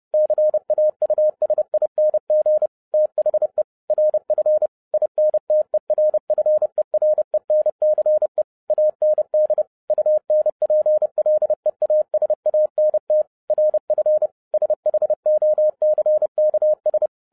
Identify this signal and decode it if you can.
Morse code, heard as a keyed tone, about 30 wpm, 610 Hz. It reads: CAUSING THE RF INTERFERENCE AND UNPLEASANT RF SHOCKS
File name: 2012-01-mp3 / n12-I DIDNT MISS THE SOLAR CYCLE 24-30wpm